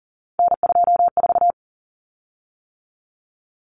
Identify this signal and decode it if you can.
Morse code, heard as a keyed tone, about 40 wpm, 700 Hz. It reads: N24
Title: Common Cut Numbers - 3-Digit 40wpm